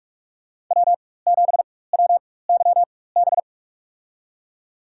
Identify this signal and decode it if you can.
Morse code, heard as a keyed tone, about 46 wpm, spaced as for 21 wpm, 700 Hz. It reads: W7WYB
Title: USA Call Signs - 5 Characters - 4x Character Spacing - 45wpm